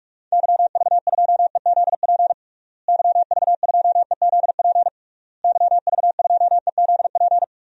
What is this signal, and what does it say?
Morse code, heard as a keyed tone, about 47 wpm, 700 Hz. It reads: YV2EZP YV2EZP YV2EZP
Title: DXCC-100 Call Signs - 6 Characters - Repeated 3x - 45wpm